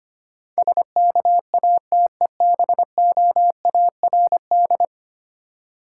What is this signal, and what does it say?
Morse code, heard as a keyed tone, about 25 wpm, 700 Hz. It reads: SKATEBOARD
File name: practice / 25wpm / Difficult Words with Hints - 25wpm